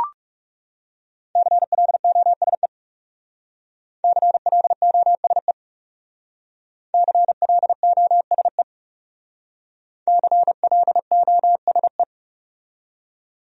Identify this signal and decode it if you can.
Morse code, heard as a keyed tone, about 36 wpm, 700 Hz. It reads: CLOSE CLOSE CLOSE CLOSE
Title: Speed-Racing Top 500 Words 30wpm